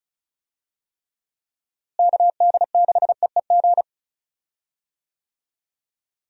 Morse code, heard as a keyed tone, about 34 wpm, 700 Hz. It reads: KD6EEG
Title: USA Call Signs - 6 Characters - Repeated 2x - 35wpm